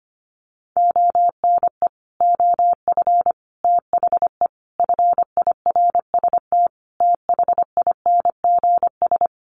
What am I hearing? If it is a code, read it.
Morse code, heard as a keyed tone, about 25 wpm, 700 Hz. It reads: ONE OF THE FIRST THINGS
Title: Sets of 5 Words Encore 25wpm